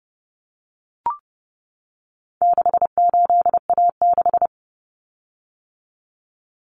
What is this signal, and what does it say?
Morse code, heard as a keyed tone, about 30 wpm, 700 Hz. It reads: 68A6
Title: Common Cut Numbers - 4-Digit 30wpm